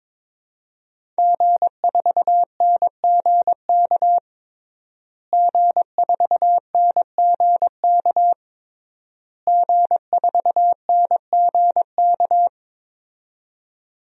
Morse code, heard as a keyed tone, about 22 wpm, 700 Hz. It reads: G4NGK G4NGK G4NGK
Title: DXCC-100 Call Signs - 5 Characters - Repeated 3x - 22wpm